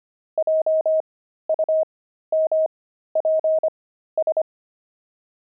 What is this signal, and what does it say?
Morse code, heard as a keyed tone, about 25 wpm, spaced as for 13 wpm, 630 Hz. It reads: JUMPS
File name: pangram-fs-13-cs-25-weaving